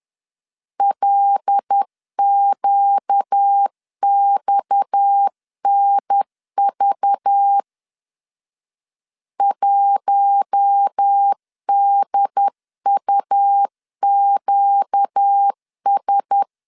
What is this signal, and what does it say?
Morse code, heard as a keyed tone, about 10 wpm, 800 Hz. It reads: LQXNV 1DUQS
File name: LEZIONI / 7 GF9Z / 7 G 10wpm 2501